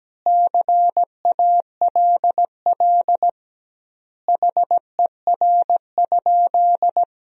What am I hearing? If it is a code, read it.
Morse code, heard as a keyed tone, about 17 wpm, 700 Hz. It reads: CALL HER?
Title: 2x Word-Spacing - Sentences from Top 100 Words - 17wpm